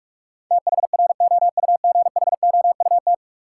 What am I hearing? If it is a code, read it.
Morse code, heard as a keyed tone, about 46 wpm, 700 Hz. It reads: THROUGHOUT
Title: Sentences from Top 800 Words - New Word Drill - 45wpm